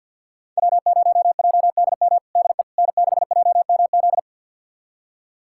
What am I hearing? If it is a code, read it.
Morse code, heard as a keyed tone, about 45 wpm, 700 Hz. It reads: W0JDM DE N6JMZ
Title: USA Callsign DE Callsign 50wpm